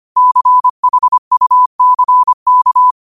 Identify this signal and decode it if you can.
Morse code, heard as a keyed tone, about 25 wpm, 1000 Hz. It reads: CHUCK
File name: qotdfiles / 25wpm